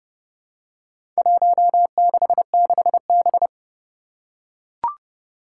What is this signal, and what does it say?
Morse code, heard as a keyed tone, about 30 wpm, 700 Hz. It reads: 166B
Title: Four Challenging Letter-Number Combinations 30wpm